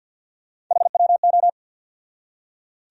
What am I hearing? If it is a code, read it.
Morse code, heard as a keyed tone, about 51 wpm, 700 Hz. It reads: HWO
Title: USA Call Sign Postfixes - 3 Letters 50wpm